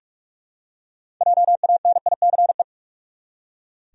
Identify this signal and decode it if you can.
Morse code, heard as a keyed tone, about 44 wpm, 700 Hz. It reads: JANICE
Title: USA Common Names - 45wpm